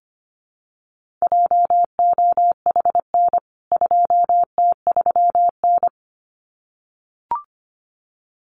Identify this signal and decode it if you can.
Morse code, heard as a keyed tone, about 24 wpm, 700 Hz. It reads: JOHN 2T3N
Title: CWT Exchange 25wpm